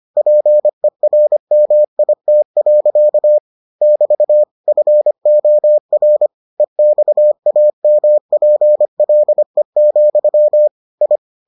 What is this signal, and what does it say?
Morse code, heard as a keyed tone, about 25 wpm, 600 Hz. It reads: PERMIT. <BT> FOR EXAMPLE, I